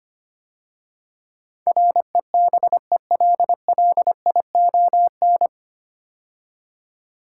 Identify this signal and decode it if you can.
Morse code, heard as a keyed tone, about 25 wpm, 700 Hz. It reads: REBELLION
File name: practice / 25wpm / 9 Letter Words - First 3 Letters Given - 25wpm